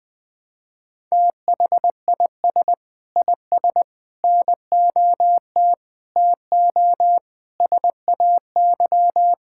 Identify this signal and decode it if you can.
Morse code, heard as a keyed tone, about 20 wpm, 700 Hz. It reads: THIS IS NOT TO SAY
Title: Sentences with 5-Word Phrases - 20wpm